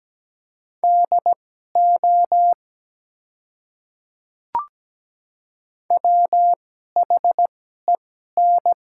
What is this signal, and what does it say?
Morse code, heard as a keyed tone, about 17 wpm, spaced as for 12 wpm, 700 Hz. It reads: DO WHEN
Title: 2x Character Spacing - Top 300 Words - 17wpm